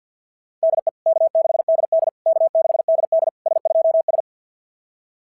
Morse code, heard as a keyed tone, about 52 wpm, 690 Hz. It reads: DE K6DD K6DD S2S
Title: SOTA QSO Practice 50wpm